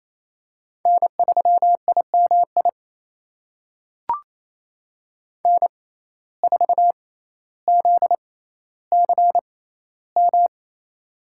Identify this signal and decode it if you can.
Morse code, heard as a keyed tone, about 28 wpm, spaced as for 14 wpm, 700 Hz. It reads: N3IMI N4ZCM
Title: USA Call Signs - 5 Characters - 6x Character Spacing - 28wpm